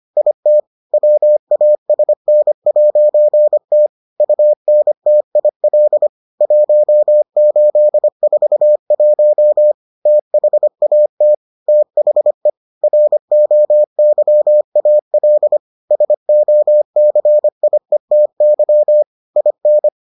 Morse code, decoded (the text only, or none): IT WASN'T UNTIL 1841 THAT THE ROYAL SOCIETY IN